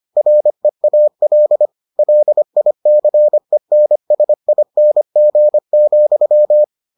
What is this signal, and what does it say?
Morse code, heard as a keyed tone, about 25 wpm, 600 Hz. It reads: REAL LICENSING,